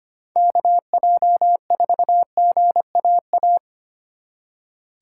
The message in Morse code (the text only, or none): KJ4GAA